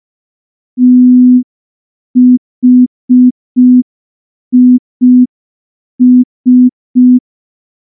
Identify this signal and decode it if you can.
Morse code, heard as a keyed tone, about 5 wpm, 250 Hz. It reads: THIS
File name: morse / morse-training-05WPM